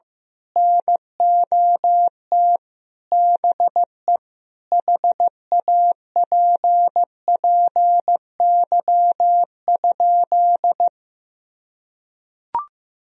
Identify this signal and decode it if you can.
Morse code, heard as a keyed tone, about 16 wpm, 700 Hz. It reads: NOT BE HAPPY?
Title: Sentences from Top 1000 Words - Review Part 2 - 15wpm